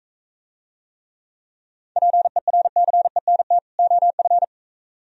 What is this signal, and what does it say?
Morse code, heard as a keyed tone, about 43 wpm, 700 Hz. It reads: PERCENT OF